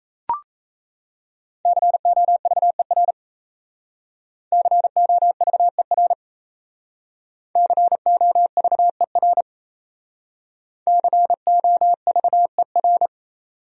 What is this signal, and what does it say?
Morse code, heard as a keyed tone, about 36 wpm, 700 Hz. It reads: COVER COVER COVER COVER